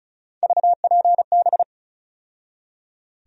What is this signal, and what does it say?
Morse code, heard as a keyed tone, about 36 wpm, 700 Hz. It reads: VPB
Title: USA Call Sign Postfixes - 3 Letters 35wpm